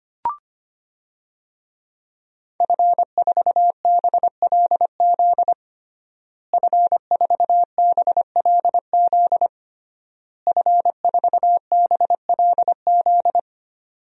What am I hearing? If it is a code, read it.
Morse code, heard as a keyed tone, about 25 wpm, 700 Hz. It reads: F4BLZ F4BLZ F4BLZ
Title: DXCC-100 Call Signs - 5 Characters - Repeated 3x - 25wpm